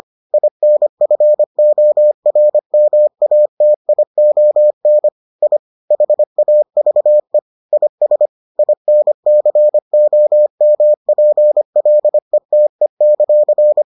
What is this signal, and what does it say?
Morse code, heard as a keyed tone, about 25 wpm, 600 Hz. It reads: INFORMATION I HAVE IS INCOMPLETE;